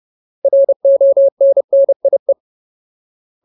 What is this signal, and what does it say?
Morse code, heard as a keyed tone, about 30 wpm, 550 Hz. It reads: RONNIE
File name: Ultimate Word List / 30WPM Ultimate